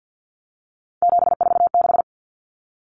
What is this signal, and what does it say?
Morse code, heard as a keyed tone, about 50 wpm, 700 Hz. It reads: 746
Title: Common Cut Numbers - 3-Digit 50wpm